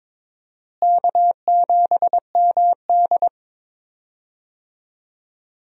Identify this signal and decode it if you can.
Morse code, heard as a keyed tone, about 22 wpm, 700 Hz. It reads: K7MD
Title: USA Call Signs - 4 Characters - Repeated 2x - 22wpm